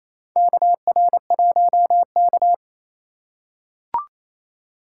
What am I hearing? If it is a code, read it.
Morse code, heard as a keyed tone, about 28 wpm, 700 Hz. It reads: KR1K
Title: USA Call Signs - 4 Characters - 4x Character Spacing - 28wpm